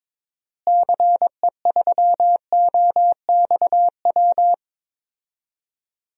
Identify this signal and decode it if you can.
Morse code, heard as a keyed tone, about 22 wpm, 700 Hz. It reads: CE3OXW
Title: DXCC-100 Call Signs - 6 Characters - Repeated 3x - 22wpm